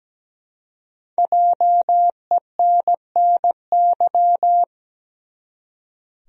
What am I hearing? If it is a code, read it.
Morse code, heard as a keyed tone, about 17 wpm, 700 Hz. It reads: JENNY
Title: USA Common Names - Max 5 Letters - 17wpm